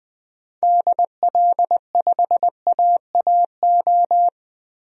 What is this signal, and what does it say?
Morse code, heard as a keyed tone, about 21 wpm, 700 Hz. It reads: DL5AAO
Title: DXCC-100 Call Signs - 6 Characters - 20wpm